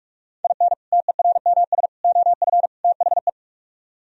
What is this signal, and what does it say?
Morse code, heard as a keyed tone, about 46 wpm, 700 Hz. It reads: IN TERMS OF THE